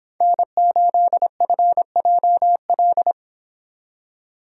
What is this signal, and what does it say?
Morse code, heard as a keyed tone, about 26 wpm, 700 Hz. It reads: N8FJL